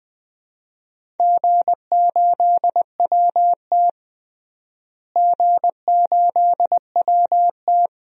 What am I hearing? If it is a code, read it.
Morse code, heard as a keyed tone, about 20 wpm, 700 Hz. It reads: G8WT G8WT
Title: DXCC-100 Call Signs - 4 Characters - Repeated 2x - 20wpm